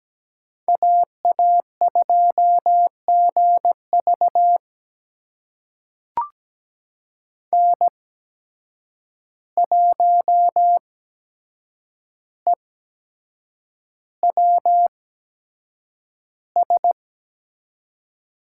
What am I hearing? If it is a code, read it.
Morse code, heard as a keyed tone, about 17 wpm, spaced as for 7 wpm, 700 Hz. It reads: AA2GV N1EWS
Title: USA Call Signs - 5 Characters - 8x Character Spacing - 17wpm